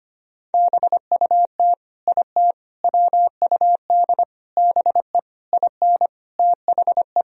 Morse code, heard as a keyed tone, about 25 wpm, 700 Hz. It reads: BUT IT WUD BE IN THE
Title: Sentences from Top 300 Words and Abbreviations 25wpm